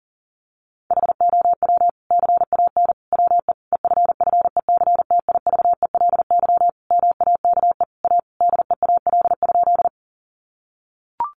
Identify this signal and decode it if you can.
Morse code, heard as a keyed tone, about 40 wpm, 700 Hz. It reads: HOW CAN WE EFFECTIVELY MAKE A DEAL?
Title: Sentences from Top 1400 Words 40wpm